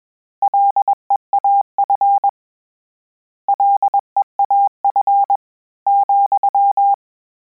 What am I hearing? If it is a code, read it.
Morse code, heard as a keyed tone, about 21 wpm, 800 Hz. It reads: LEAF LEAF,
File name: Combinations / Combinations-mp3s / Combinations@20